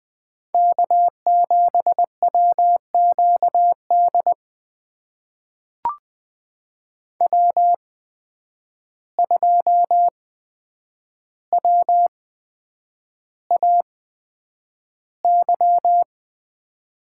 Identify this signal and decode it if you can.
Morse code, heard as a keyed tone, about 20 wpm, spaced as for 9 wpm, 700 Hz. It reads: K7WQD W2WAY
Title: USA Call Signs - 8x Character Spacing - 20wpm